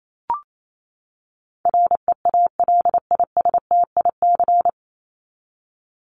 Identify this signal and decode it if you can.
Morse code, heard as a keyed tone, about 28 wpm, 700 Hz. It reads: REALISTIC
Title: Top 1751-2000 Words 28wpm